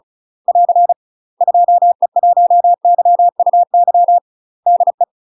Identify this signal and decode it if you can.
Morse code, heard as a keyed tone, about 36 wpm, 700 Hz. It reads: <AR> 2E1YUY DE
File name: UK MASTER QSOs TRAINING / 35WPM / UK MASTER QSOs 0031